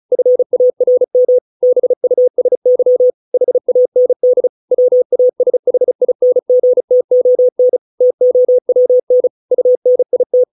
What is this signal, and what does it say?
Morse code, heard as a keyed tone, about 35 wpm, 500 Hz. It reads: FARM BUSY HAND WASHINGTON TOWN UNIT